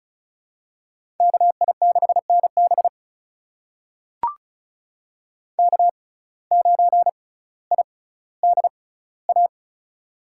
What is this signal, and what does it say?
Morse code, heard as a keyed tone, about 36 wpm, spaced as for 18 wpm, 700 Hz. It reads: KI6NB K9IDA
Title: USA Call Signs - 5 Characters - 6x Character Spacing - 35wpm